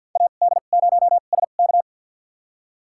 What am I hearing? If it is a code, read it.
Morse code, heard as a keyed tone, about 51 wpm, spaced as for 36 wpm, 700 Hz. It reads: AD0SK
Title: USA Call Signs - 2x Character Spacing - 50wpm